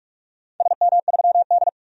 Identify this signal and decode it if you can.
Morse code, heard as a keyed tone, about 47 wpm, 700 Hz. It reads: SM3D
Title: DXCC-100 Call Signs - 4 Characters - 45wpm